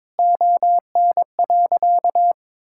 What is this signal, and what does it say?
Morse code, heard as a keyed tone, about 22 wpm, 700 Hz. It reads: ON.
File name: practice / 22wpm / Sentences with 4-Word Phrases - 22wpm